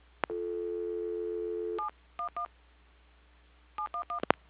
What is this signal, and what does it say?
DTMF, heard as telephone touch tones, tones of about 110 ms. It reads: 711 711
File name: recordings / PortlandUnion Millennium